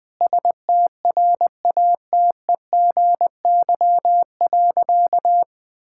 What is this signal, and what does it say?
Morse code, heard as a keyed tone, about 20 wpm, 700 Hz. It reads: STRATEGY.
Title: Sentences from Top 1000 Words - Review Part 5 - 20wpm